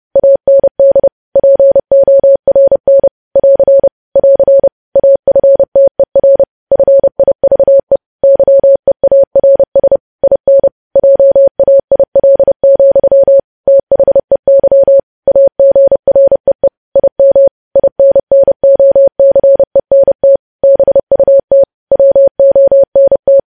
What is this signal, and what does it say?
Morse code, heard as a keyed tone, about 30 wpm, 550 Hz. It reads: AND PORN <AR> <AR> AFTER FIVE YEARS IN JAIL, THEY AGREE IM INNOCENT BUT WONT
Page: news headlines in morse code at 30 wpm